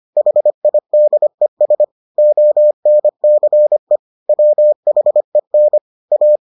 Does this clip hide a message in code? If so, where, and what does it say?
Yes: Morse code from 0.2 to 6.4 s, heard as a keyed tone, about 25 wpm, 600 Hz. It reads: HIDES ONCE WHEN A